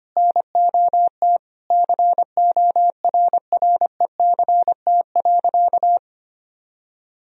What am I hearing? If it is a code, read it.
Morse code, heard as a keyed tone, about 25 wpm, 700 Hz. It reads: NOT CORRECT.
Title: Sentences from Top 1000 Words - Review Part 3 - 25wpm